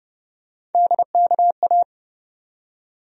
Morse code, heard as a keyed tone, about 31 wpm, 700 Hz. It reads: DKA
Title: USA Call Sign Postfixes - 3 Letters 30wpm